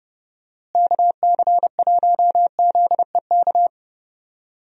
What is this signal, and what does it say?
Morse code, heard as a keyed tone, about 31 wpm, 700 Hz. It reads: KC1ZEK
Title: USA Call Signs - 6 Characters - Repeated 3x - 30wpm